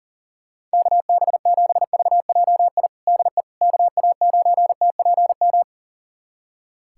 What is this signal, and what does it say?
Morse code, heard as a keyed tone, about 40 wpm, 700 Hz. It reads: KB7VJI DE KA9TPM
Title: USA Callsign DE Callsign - 6 Characters - 40wpm